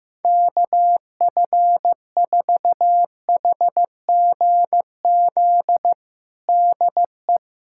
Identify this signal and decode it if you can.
Morse code, heard as a keyed tone, about 15 wpm, 700 Hz. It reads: KF4HGZ DE
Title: USA Callsign DE Callsign - 6 Characters - 15wpm